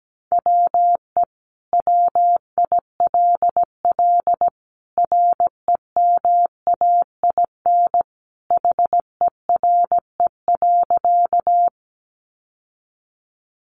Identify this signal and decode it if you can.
Morse code, heard as a keyed tone, about 17 wpm, 700 Hz. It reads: WE WILL REMAIN HERE.
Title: Sentences from Top 1200 Words 17wpm